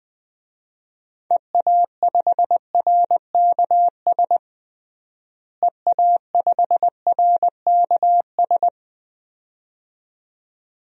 Morse code, heard as a keyed tone, about 20 wpm, 700 Hz. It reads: EA5RKS EA5RKS
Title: DXCC-100 Call Signs - 6 Characters - Repeated 2x - 20wpm